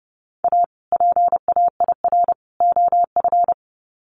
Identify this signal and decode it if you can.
Morse code, heard as a keyed tone, about 30 wpm, 700 Hz. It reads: A PAIR OF
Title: Sets of 3 Words 30wpm